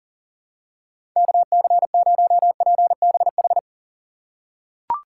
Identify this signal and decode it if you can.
Morse code, heard as a keyed tone, about 40 wpm, 700 Hz. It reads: KC0PBH
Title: USA Call Signs - 6 Characters - 2x Character Spacing - 40wpm